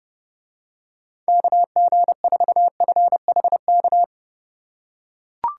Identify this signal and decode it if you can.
Morse code, heard as a keyed tone, about 29 wpm, 700 Hz. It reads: KG4FHK